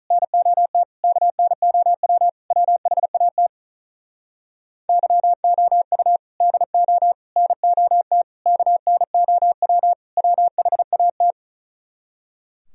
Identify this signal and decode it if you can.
Morse code, heard as a keyed tone, about 38 wpm, 700 Hz. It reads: NOT KNOW WHAT YOU DO NOT KNOW WHAT
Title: Speed-Racing - Sets of 5 Words 35wpm